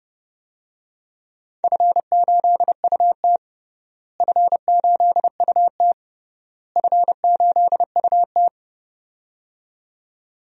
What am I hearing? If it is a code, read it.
Morse code, heard as a keyed tone, about 31 wpm, 700 Hz. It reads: F8UT F8UT F8UT
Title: DXCC-100 Call Signs - 4 Characters - Repeated 3x - 30wpm